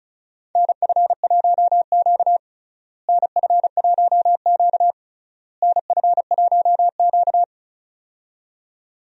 Morse code, heard as a keyed tone, about 36 wpm, 700 Hz. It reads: NF1Q NF1Q NF1Q
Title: USA Call Signs - 4 Characters - Repeated 3x - 35wpm